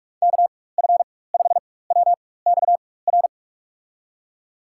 Morse code, heard as a keyed tone, about 45 wpm, spaced as for 21 wpm, 700 Hz. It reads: KF5WXR
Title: USA Call Signs - 6 Characters - 4x Character Spacing - 45wpm